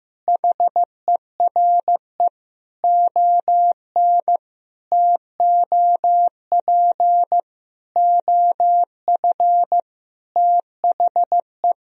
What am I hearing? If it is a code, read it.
Morse code, heard as a keyed tone, about 15 wpm, 700 Hz. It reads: HERE ON TOP OF THE
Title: Sentences from Top 1000 Words - Review Part 1 - 15wpm